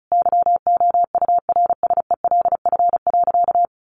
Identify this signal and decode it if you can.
Morse code, heard as a keyed tone, about 34 wpm, 700 Hz. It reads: YOURSELF.
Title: Sentences from Top 600 Words 35wpm